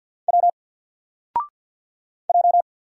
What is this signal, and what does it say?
Morse code, heard as a keyed tone, about 50 wpm, 700 Hz. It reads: W J
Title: Single Letter-Number Alt 50wpm